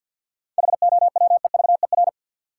Morse code, heard as a keyed tone, about 52 wpm, 700 Hz. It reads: HOWEVER
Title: Sentences from Top 600 Words - New Word Drill - 50wpm